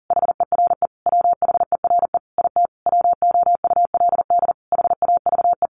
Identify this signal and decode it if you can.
Morse code, heard as a keyed tone, about 40 wpm, 700 Hz. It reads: HERE WHERE IT WOULD HAVE
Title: A Sound Of Thunder 40wpm